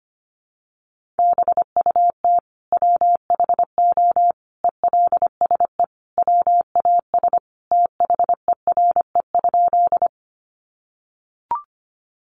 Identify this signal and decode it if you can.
Morse code, heard as a keyed tone, about 25 wpm, 700 Hz. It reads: BUT WHO ELSE WAS THERE?
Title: Sentences from Top 800 Words 25wpm